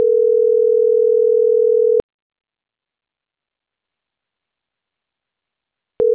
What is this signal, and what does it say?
Ringback tone (North America), 440 + 480 Hz.